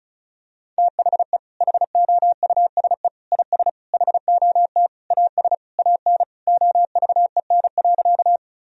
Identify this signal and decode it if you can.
Morse code, heard as a keyed tone, about 34 wpm, 700 Hz. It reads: THE HOUSE IS HOT AS AN OVEN.
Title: Sentences from Top 1000 Words - Review Part 4 - 35wpm